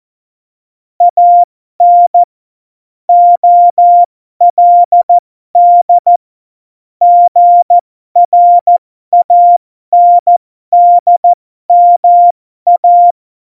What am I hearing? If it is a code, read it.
Morse code, heard as a keyed tone, about 14 wpm, spaced as for 12 wpm, 700 Hz. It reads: AN OLD GRANDMA
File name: Morsekurs / Witze